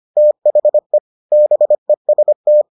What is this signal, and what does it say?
Morse code, heard as a keyed tone, about 25 wpm, 600 Hz. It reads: THE BEST